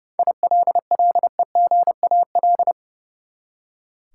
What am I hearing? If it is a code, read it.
Morse code, heard as a keyed tone, about 30 wpm, 700 Hz. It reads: ILLEGAL
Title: Seven Letter Words - 30wpm